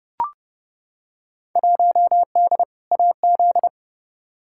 Morse code, heard as a keyed tone, about 30 wpm, 700 Hz. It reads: 1D AZ
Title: Field Day Exchanges - 30wpm